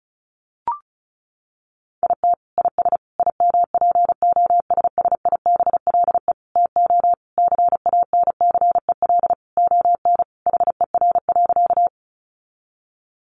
Morse code, heard as a keyed tone, about 34 wpm, 700 Hz. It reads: IT IS IMPOSSIBLE TO CANCEL ON HER.
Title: Sentences from Top 1200 Words 35wpm